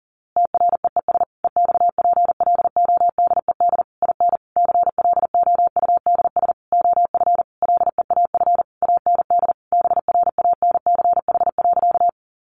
Morse code, heard as a keyed tone, about 40 wpm, 700 Hz. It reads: TREES EXPLODED IN CLOUDS OF LEAF AND BRANCH.
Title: A Sound Of Thunder 40wpm